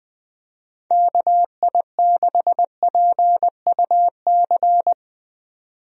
Morse code, heard as a keyed tone, about 20 wpm, 700 Hz. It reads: KI6PUC
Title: USA Call Signs - 6 Characters - 20wpm